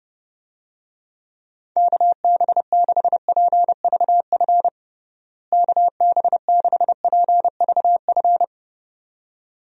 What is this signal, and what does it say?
Morse code, heard as a keyed tone, about 30 wpm, 700 Hz. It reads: KB6PVF KB6PVF
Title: USA Call Signs - 6 Characters - Repeated 2x - 30wpm